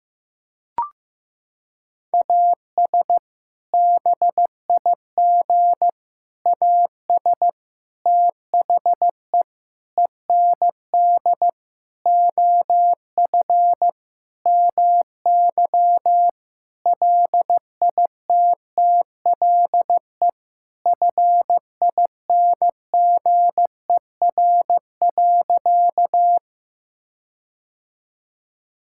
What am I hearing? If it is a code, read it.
Morse code, heard as a keyed tone, about 15 wpm, 700 Hz. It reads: AS BIG AS THE END OF MY LITTLE FINGER.
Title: Sentences from Top 1600 Words 15wpm